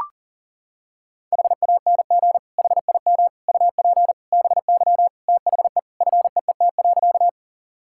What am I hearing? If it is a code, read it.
Morse code, heard as a keyed tone, about 40 wpm, 700 Hz. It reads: HANG HIM UP BY THE FEET.
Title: Sentences from Top 1600 Words 40wpm